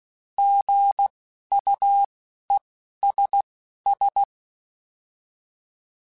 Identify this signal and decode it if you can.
Morse code, heard as a keyed tone, about 16 wpm, spaced as for 11 wpm, 800 Hz. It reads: GUESS
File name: Alphabetical / Alphabetical mp3s / Gs @10